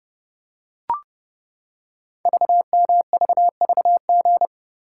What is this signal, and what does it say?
Morse code, heard as a keyed tone, about 30 wpm, 700 Hz. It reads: VMVVG
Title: Code Groups - 5 Letters - 30wpm